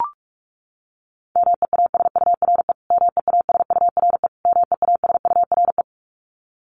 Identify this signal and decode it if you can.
Morse code, heard as a keyed tone, about 44 wpm, 700 Hz. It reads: MEASURE MEASURE MEASURE
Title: Top 500 Words Repeated 45wpm